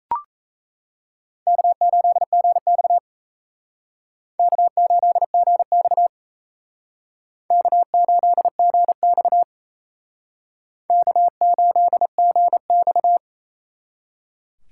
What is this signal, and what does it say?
Morse code, heard as a keyed tone, about 36 wpm, 700 Hz. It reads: K8GX K8GX K8GX K8GX